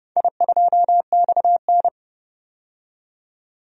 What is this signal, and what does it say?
Morse code, heard as a keyed tone, about 30 wpm, 700 Hz. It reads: I2XN
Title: DXCC-100 Call Signs - 4 Characters - 30wpm